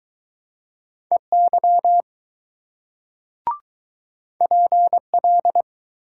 Morse code, heard as a keyed tone, about 21 wpm, 700 Hz. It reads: EY PL